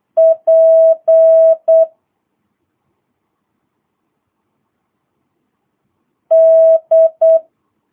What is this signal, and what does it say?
Morse code, heard as a keyed tone, about 8 wpm, 650 Hz. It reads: P D